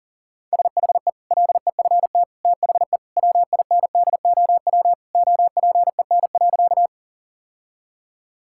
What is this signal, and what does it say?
Morse code, heard as a keyed tone, about 40 wpm, 700 Hz. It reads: SHE LEFT THE WINDOW OPEN.
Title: Sentences from Top 1600 Words 40wpm